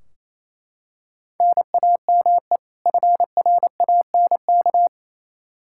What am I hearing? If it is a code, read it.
Morse code, heard as a keyed tone, about 28 wpm, 700 Hz. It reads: NAME FRANK
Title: Practice QSO - Exchange 1 - Beginning - 28wpm